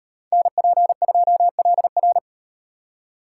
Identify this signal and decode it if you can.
Morse code, heard as a keyed tone, about 38 wpm, 700 Hz. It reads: NP2LR